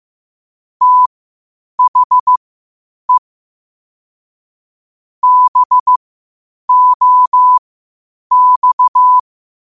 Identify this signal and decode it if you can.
Morse code, heard as a keyed tone, about 15 wpm, spaced as for 8 wpm, 1000 Hz. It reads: THE BOX
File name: qotdfiles / 8wpm